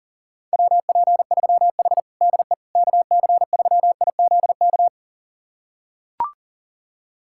Morse code, heard as a keyed tone, about 40 wpm, 700 Hz. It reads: WP3H DE KC3IZK
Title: USA Callsign DE Callsign 40wpm